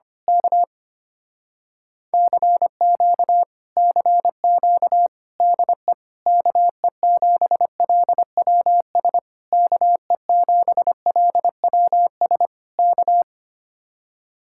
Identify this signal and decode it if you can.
Morse code, heard as a keyed tone, about 24 wpm, 700 Hz. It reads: K CQ CQ DE KE7LWS KE7LWS K
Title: Speed-Racing Common QSO Phrases - 25wpm